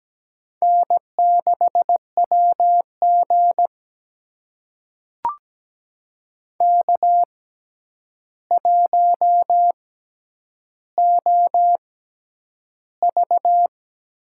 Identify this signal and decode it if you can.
Morse code, heard as a keyed tone, about 17 wpm, spaced as for 9 wpm, 700 Hz. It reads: N6WG K1OV
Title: USA Call Signs - 4 Characters - 6x Character Spacing - 17wpm